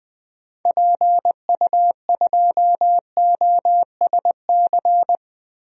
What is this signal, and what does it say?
Morse code, heard as a keyed tone, about 21 wpm, 700 Hz. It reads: PU2OSC
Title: DXCC-100 Call Signs - 6 Characters - Repeated 2x - 20wpm